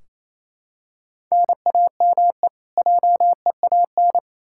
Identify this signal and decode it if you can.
Morse code, heard as a keyed tone, about 28 wpm, 700 Hz. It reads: NAME JEAN
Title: Practice QSO - Exchange 1 - Beginning - 28wpm